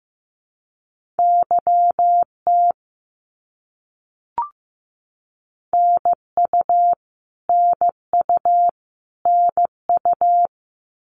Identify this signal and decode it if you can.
Morse code, heard as a keyed tone, about 15 wpm, 700 Hz. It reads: YT NU NU NU
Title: Canadian Provinces and Territories - Repeated 3x - 15wpm